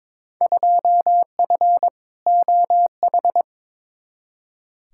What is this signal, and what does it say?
Morse code, heard as a keyed tone, about 21 wpm, 700 Hz. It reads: 2F OH
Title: Field Day Exchanges - 22wpm